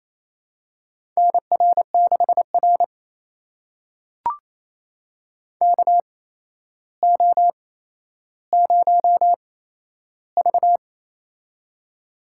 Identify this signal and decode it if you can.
Morse code, heard as a keyed tone, about 28 wpm, spaced as for 12 wpm, 700 Hz. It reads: NR6R KO0V
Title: USA Call Signs - 4 Characters - 8x Character Spacing - 28wpm